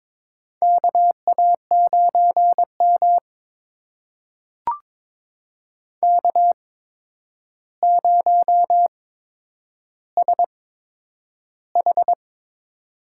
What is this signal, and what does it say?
Morse code, heard as a keyed tone, about 22 wpm, spaced as for 9 wpm, 700 Hz. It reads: KA9M K0SH
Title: USA Call Signs - 4 Characters - 8x Character Spacing - 22wpm